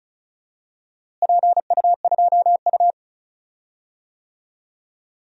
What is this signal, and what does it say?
Morse code, heard as a keyed tone, about 36 wpm, 700 Hz. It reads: PU2U
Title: DXCC-100 Call Signs - 4 Characters - Repeated 3x - 35wpm